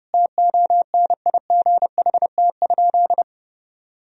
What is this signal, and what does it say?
Morse code, heard as a keyed tone, about 32 wpm, 700 Hz. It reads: TONIGHT?
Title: Sentences from Top 1000 Words - Review Part 3 - 30wpm